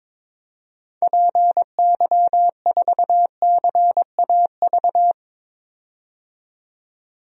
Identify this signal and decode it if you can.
Morse code, heard as a keyed tone, about 22 wpm, 700 Hz. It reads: PY4CAV
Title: DXCC-100 Call Signs - 6 Characters - 22wpm